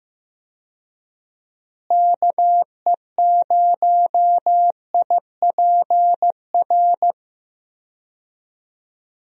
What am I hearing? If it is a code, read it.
Morse code, heard as a keyed tone, about 15 wpm, 700 Hz. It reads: KE0IPR
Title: USA Call Signs - 6 Characters - Repeated 3x - 15wpm